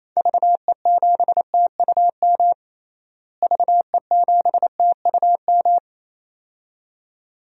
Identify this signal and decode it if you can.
Morse code, heard as a keyed tone, about 28 wpm, 700 Hz. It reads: VE7TUM VE7TUM
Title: DXCC-100 Call Signs - 6 Characters - Repeated 2x - 28wpm